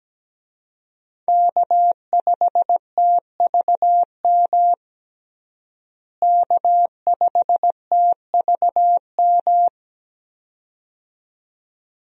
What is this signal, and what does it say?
Morse code, heard as a keyed tone, about 17 wpm, 700 Hz. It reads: K5TVM K5TVM
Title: USA Call Signs - 5 Characters - Repeated 2x - 17wpm